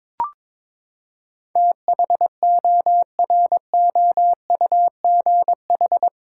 Morse code, heard as a keyed tone, about 21 wpm, 700 Hz. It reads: THOROUGH
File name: practice / 22wpm / Eight Letter Words - 22wpm